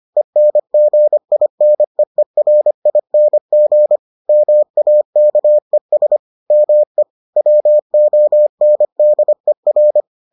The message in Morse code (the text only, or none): ENGINEERING MAKES ME WONDER